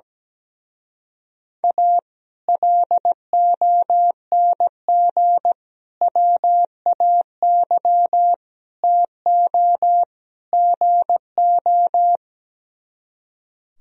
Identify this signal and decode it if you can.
Morse code, heard as a keyed tone, about 17 wpm, 700 Hz. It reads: A LONG WAY TO GO
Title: Speed-Racing - Sets of 5 Words 17wpm